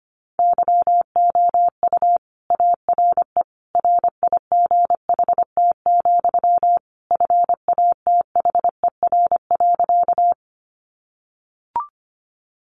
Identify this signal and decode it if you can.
Morse code, heard as a keyed tone, about 25 wpm, 700 Hz. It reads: YOU ARE RIGHT, FATHER.
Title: Sentences from Top 300 Words 25wpm